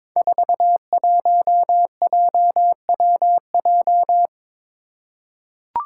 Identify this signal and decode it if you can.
Morse code, heard as a keyed tone, about 22 wpm, 700 Hz. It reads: 41JWJ
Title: Five Challenging Letter-Number Combinations 22wpm